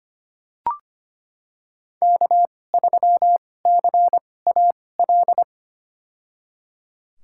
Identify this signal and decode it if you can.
Morse code, heard as a keyed tone, about 25 wpm, spaced as for 18 wpm, 700 Hz. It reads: K3CAL
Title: USA Call Signs - 5 Characters - 2x Character Spacing - 25wpm